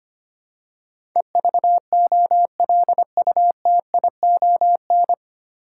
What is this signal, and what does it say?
Morse code, heard as a keyed tone, about 25 wpm, 700 Hz. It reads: EVOLUTION
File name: Nine Letter Words - 25wpm